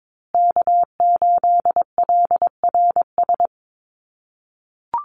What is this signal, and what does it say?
Morse code, heard as a keyed tone, about 22 wpm, 700 Hz. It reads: K8LRS
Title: USA Call Signs Repeated 2x 22wpm